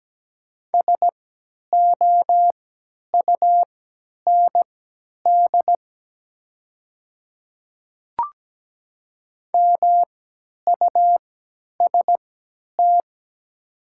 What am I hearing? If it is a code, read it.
Morse code, heard as a keyed tone, about 17 wpm, spaced as for 10 wpm, 700 Hz. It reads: SOUND MUST
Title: 3x Character Spacing - Top 200 Words - 17wpm